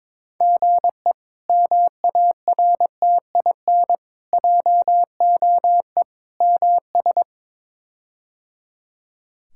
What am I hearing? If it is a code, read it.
Morse code, heard as a keyed tone, about 23 wpm, 700 Hz. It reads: GE MARTIN JOE MS
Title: K1USN SST QSO Practice - 22wpm